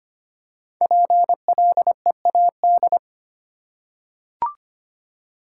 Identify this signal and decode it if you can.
Morse code, heard as a keyed tone, about 25 wpm, 700 Hz. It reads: PLEAD
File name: 5 Letter Words - First 3 Letters Given - 25wpm